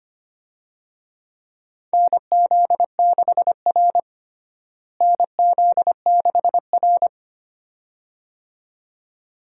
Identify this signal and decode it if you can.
Morse code, heard as a keyed tone, about 25 wpm, 700 Hz. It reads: NZ6R NZ6R
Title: USA Call Signs - 4 Characters - Repeated 2x - 25wpm